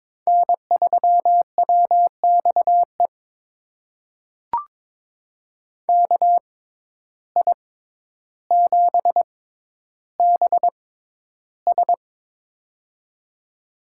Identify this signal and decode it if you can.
Morse code, heard as a keyed tone, about 22 wpm, spaced as for 11 wpm, 700 Hz. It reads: N3WXE KI7BS
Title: USA Call Signs - 5 Characters - 6x Character Spacing - 22wpm